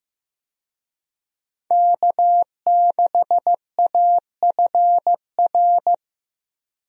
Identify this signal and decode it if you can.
Morse code, heard as a keyed tone, about 15 wpm, 700 Hz. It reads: K6AFR
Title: USA Call Signs - 5 Characters - Repeated 3x - 15wpm